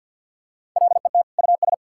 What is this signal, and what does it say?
Morse code, heard as a keyed tone, about 50 wpm, 700 Hz. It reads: LET US
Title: Sentences from Top 1600 Words 50wpm